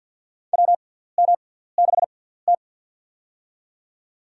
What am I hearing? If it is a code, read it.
Morse code, heard as a keyed tone, about 53 wpm, spaced as for 17 wpm, 700 Hz. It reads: WM6T
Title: USA Call Signs - 6x Character Spacing - 50wpm